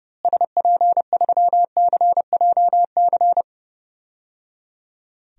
Morse code, heard as a keyed tone, about 30 wpm, 700 Hz. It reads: SP3CJC
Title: DXCC-100 Call Signs - 6 Characters - Repeated 3x - 30wpm